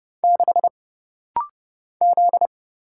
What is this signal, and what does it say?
Morse code, heard as a keyed tone, about 30 wpm, 700 Hz. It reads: 6 Z
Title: Single Letter-Number - Rapid-Fire 30wpm